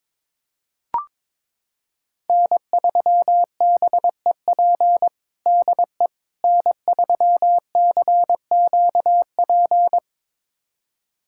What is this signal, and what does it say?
Morse code, heard as a keyed tone, about 22 wpm, 700 Hz. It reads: N3BEP DE N3CQP
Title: USA Callsign DE Callsign - 5 Characters - 22wpm